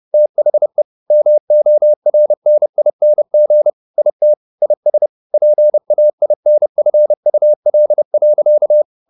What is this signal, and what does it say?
Morse code, heard as a keyed tone, about 30 wpm, 600 Hz. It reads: THE MORNING IT IS PAINFUL.
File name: the old man and the sea / the old man and the sea 1 30wpm